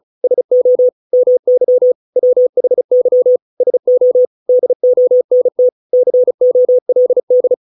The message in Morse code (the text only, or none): SO MY WHY SO DONT COLD